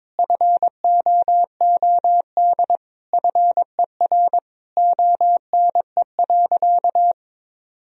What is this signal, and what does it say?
Morse code, heard as a keyed tone, about 21 wpm, 700 Hz. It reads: FOOD FER ONE.
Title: Sentences from Top 300 Words and Abbreviations 22wpm